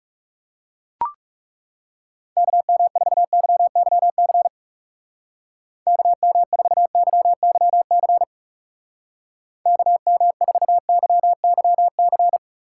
Morse code, heard as a keyed tone, about 37 wpm, 700 Hz. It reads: KM4YYC KM4YYC KM4YYC
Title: Speed Racing - USA Call Signs 30wpm